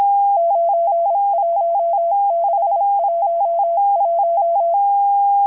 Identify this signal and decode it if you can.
Morse code, heard as a keyed tone, about 26 wpm, 680 Hz. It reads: 91611
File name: HM02- -0437utc-7351khz